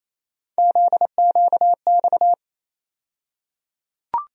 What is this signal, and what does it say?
Morse code, heard as a keyed tone, about 28 wpm, 700 Hz. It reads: ZQX
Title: USA Call Sign Postfixes - 3 Letters 28wpm